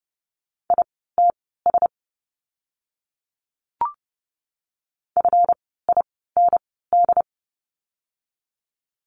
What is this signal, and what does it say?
Morse code, heard as a keyed tone, about 30 wpm, spaced as for 17 wpm, 700 Hz. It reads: ITS FIND
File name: course / 30:10wpm / 230 Lesson 37f - Letter x - Words 30wpm